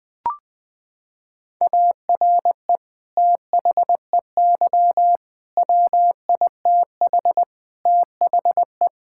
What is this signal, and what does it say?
Morse code, heard as a keyed tone, about 20 wpm, 700 Hz. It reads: ARE THEY WITH THE
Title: Sentences from Top 1000 Words - Review Part 5 - 20wpm